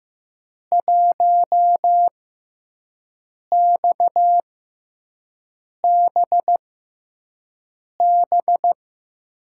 Morse code, heard as a keyed tone, about 15 wpm, spaced as for 5 wpm, 700 Hz. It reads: 1XBB
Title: USA Call Signs - 5 Characters - 6x Character Spacing - 15wpm